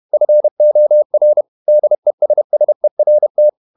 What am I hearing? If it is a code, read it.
Morse code, heard as a keyed tone, about 31 wpm, 600 Hz. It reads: FOR DESSERT
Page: Never 31wpm | CW med Gnister